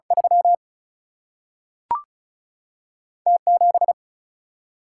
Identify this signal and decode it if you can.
Morse code, heard as a keyed tone, about 36 wpm, 700 Hz. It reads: 3 T 7
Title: Single-Letter Single-Number 35wpm